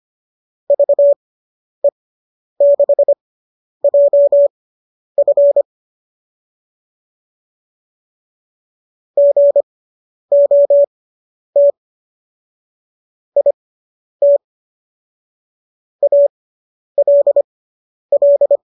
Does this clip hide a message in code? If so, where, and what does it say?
Yes: Morse code from 0.7 to 18.6 s, heard as a keyed tone, about 25 wpm, spaced as for 10 wpm, 570 Hz. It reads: VE6JF GOT IT ALL